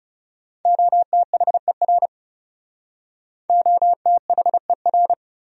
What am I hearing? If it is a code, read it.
Morse code, heard as a keyed tone, about 39 wpm, 700 Hz. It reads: OTHER OTHER